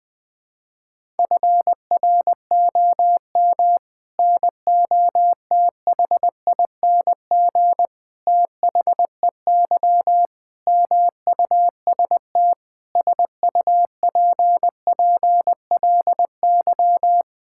Morse code, heard as a keyed tone, about 20 wpm, 700 Hz. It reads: FROM NOTHING THEY MUST SUPPLY
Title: Sentences from Top 1000 Words - Review Part 5 - 20wpm